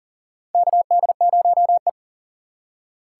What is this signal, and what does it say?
Morse code, heard as a keyed tone, about 39 wpm, 700 Hz. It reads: KD0E
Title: USA Call Signs - 4 Characters - 4x Character Spacing - 40wpm